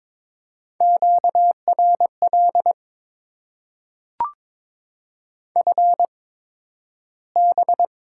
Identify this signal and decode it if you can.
Morse code, heard as a keyed tone, about 22 wpm, spaced as for 12 wpm, 690 Hz. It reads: QRL FB
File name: Top 90 QSO Elements - 8x Character Spacing - 22wpm